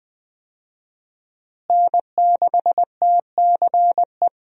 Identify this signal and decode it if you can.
Morse code, heard as a keyed tone, about 20 wpm, 700 Hz. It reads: N6TCE
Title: USA Call Signs - 5 Characters - Repeated 3x - 20wpm